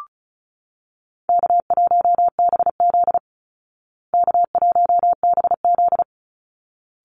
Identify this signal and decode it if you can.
Morse code, heard as a keyed tone, about 34 wpm, 700 Hz. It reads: K1BZ K1BZ
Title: Easy Call Signs Repeated 2x - 35wpm